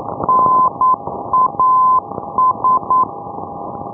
Morse code, heard as a keyed tone, about 9 wpm, 1020 Hz, with noise. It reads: NAS
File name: ndb punta arenas